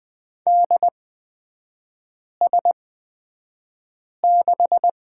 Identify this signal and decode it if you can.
Morse code, heard as a keyed tone, about 20 wpm, spaced as for 5 wpm, 700 Hz. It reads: DS6
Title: 224 Lesson 36e - BT - Three Characters 20wpm